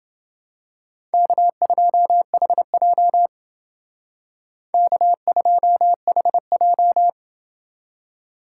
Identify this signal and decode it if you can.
Morse code, heard as a keyed tone, about 28 wpm, 700 Hz. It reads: K2HJ K2HJ